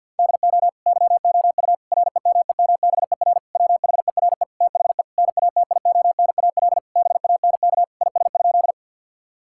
Morse code, heard as a keyed tone, about 50 wpm, 700 Hz. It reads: DO YOU REMEMBER WHERE THE NATIONAL BANK IS?
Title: Sentences from Top 1000 Words - Review Part 6 - 50wpm